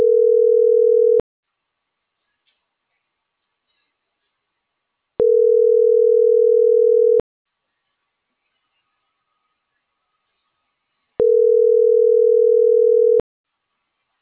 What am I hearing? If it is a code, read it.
Ringback tone (North America), 440 + 480 Hz.